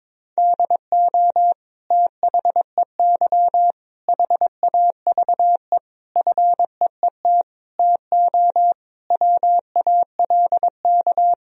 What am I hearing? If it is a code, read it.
Morse code, heard as a keyed tone, about 22 wpm, 700 Hz. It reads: DO THEY HAVE FEET TO WALK
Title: Sentences from Top 1000 Words - Review Part 1 - 22wpm